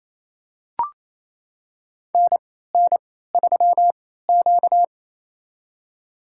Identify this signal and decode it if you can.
Morse code, heard as a keyed tone, about 27 wpm, spaced as for 16 wpm, 700 Hz. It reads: NN3Q
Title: USA Call Signs - 4 Characters - 3x Character Spacing - 28wpm